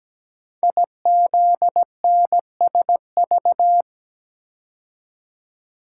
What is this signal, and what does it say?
Morse code, heard as a keyed tone, about 17 wpm, 700 Hz. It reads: IZNSV
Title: Code Groups - 5 Letters - 17wpm